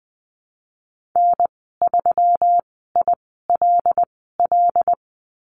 Morse code, heard as a keyed tone, about 20 wpm, spaced as for 15 wpm, 700 Hz. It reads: N3ILL
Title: 122 Lesson 20g - Number 3 - Call Signs 20wpm